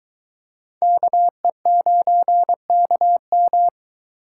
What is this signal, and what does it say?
Morse code, heard as a keyed tone, about 23 wpm, 700 Hz. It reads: KE9KM